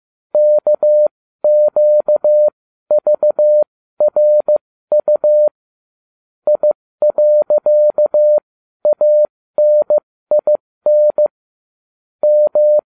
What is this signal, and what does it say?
Morse code, heard as a keyed tone, about 15 wpm, spaced as for 13 wpm, 600 Hz. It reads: KQVRU I.ANIN M